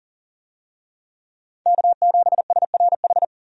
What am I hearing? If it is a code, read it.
Morse code, heard as a keyed tone, about 40 wpm, 700 Hz. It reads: K7SRH
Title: USA Call Signs - 5 Characters - Repeated 3x - 40wpm